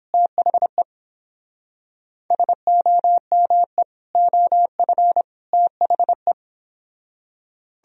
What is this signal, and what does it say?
Morse code, heard as a keyed tone, about 30 wpm, 700 Hz. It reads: THE SOME OF THE